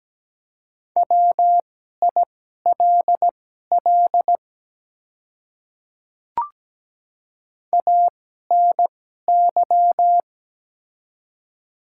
Morse code, heard as a keyed tone, about 17 wpm, spaced as for 12 wpm, 700 Hz. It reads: WILL ANY
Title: 2x Character Spacing - Top 200 Words - 17wpm